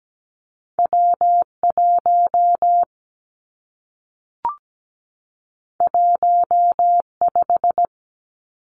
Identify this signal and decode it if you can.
Morse code, heard as a keyed tone, about 17 wpm, 700 Hz. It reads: W1 15
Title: Two Challenging Letter-Number Combinations 17wpm